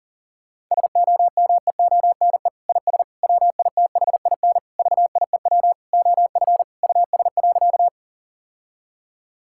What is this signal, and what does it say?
Morse code, heard as a keyed tone, about 40 wpm, 700 Hz. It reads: SOMEONE IS WITHIN VIEW OF US.
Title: Sentences from Top 1000 Words - Review Part 6 - 40wpm